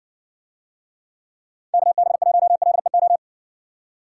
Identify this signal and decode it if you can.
Morse code, heard as a keyed tone, about 60 wpm, 700 Hz. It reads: KB1LEO